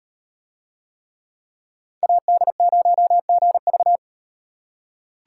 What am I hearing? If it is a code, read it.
Morse code, heard as a keyed tone, about 38 wpm, 700 Hz. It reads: AD0GV